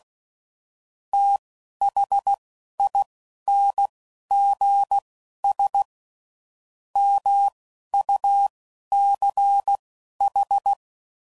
Morse code, heard as a keyed tone, about 16 wpm, spaced as for 11 wpm, 800 Hz. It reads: THINGS MUCH
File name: CW Code Groups @10WPM/Technical Subjects